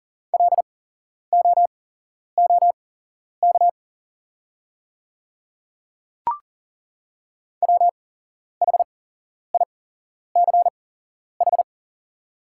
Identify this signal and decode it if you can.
Morse code, heard as a keyed tone, about 40 wpm, spaced as for 11 wpm, 690 Hz. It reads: LOOK WHICH
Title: 8x Character Spacing - Top 100 Words - 40wpm